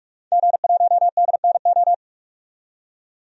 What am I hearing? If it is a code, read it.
Morse code, heard as a keyed tone, about 46 wpm, 700 Hz. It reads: G1DNO
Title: DXCC-100 Call Signs - 5 Characters - 45wpm